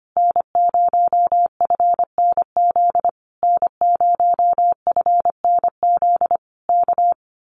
Morse code, heard as a keyed tone, about 25 wpm, 700 Hz. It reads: N0FNZ N0FNZ K
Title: Common QSO Phrases Encore 25wpm